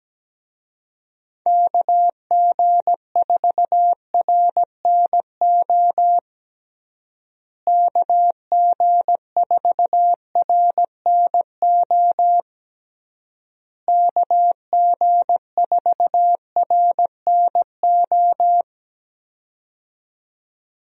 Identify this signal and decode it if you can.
Morse code, heard as a keyed tone, about 17 wpm, 700 Hz. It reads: KG4RNO KG4RNO KG4RNO
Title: USA Call Signs - 6 Characters - Repeated 3x - 17wpm